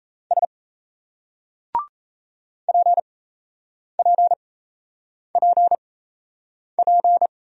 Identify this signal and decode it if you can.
Morse code, heard as a keyed tone, about 35 wpm, spaced as for 8 wpm, 700 Hz. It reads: S PPPP
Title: Speed-Racing Single Letters 28wpm